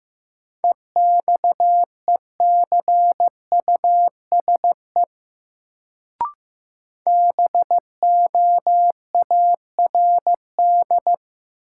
Morse code, heard as a keyed tone, about 16 wpm, 700 Hz. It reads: EXECUSE BOARD
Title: Sentences from Top 700 Words - New Word Drill - 15wpm